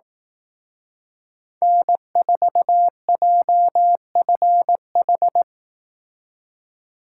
Morse code, heard as a keyed tone, about 18 wpm, 700 Hz. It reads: N4JFH
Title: Speed Racing - USA 1x3 Call Signs 15wpm